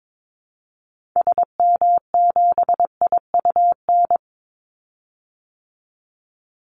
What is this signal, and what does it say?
Morse code, heard as a keyed tone, about 22 wpm, 700 Hz. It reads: SM7IUN
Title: Call Signs Repeated 3x - 22wpm